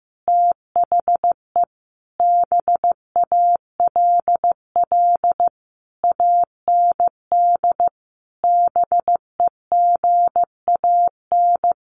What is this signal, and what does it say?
Morse code, heard as a keyed tone, about 15 wpm, 700 Hz. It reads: THE BALL AND BEGAN
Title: Sentences from Top 900 Words 15wpm